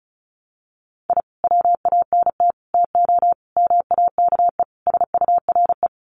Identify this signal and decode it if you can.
Morse code, heard as a keyed tone, about 34 wpm, 700 Hz. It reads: I WANT TO MAKE SURE
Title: Sets of 5 Words Encore2 35wpm